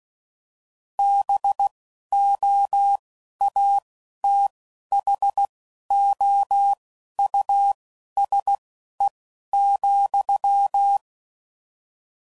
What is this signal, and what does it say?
Morse code, heard as a keyed tone, about 16 wpm, spaced as for 11 wpm, 800 Hz. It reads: BOATHOUSE,
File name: Suffixes / General / Houses